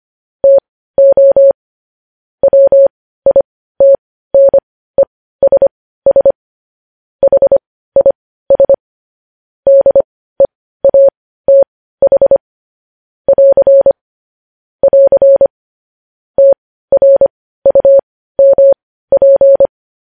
Morse code, heard as a keyed tone, about 25 wpm, spaced as for 15 wpm, 550 Hz. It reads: TO WITNESS HIS DEATH <AR> <AR> TRUMP
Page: news headlines in morse code at 15 wpm